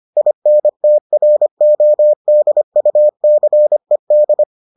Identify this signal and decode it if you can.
Morse code, heard as a keyed tone, about 25 wpm, 600 Hz. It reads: INTRODUCED